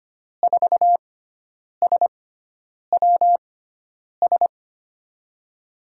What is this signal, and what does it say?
Morse code, heard as a keyed tone, about 25 wpm, spaced as for 9 wpm, 700 Hz. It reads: 4SWS
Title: USA Call Signs - 5 Characters - 6x Character Spacing - 25wpm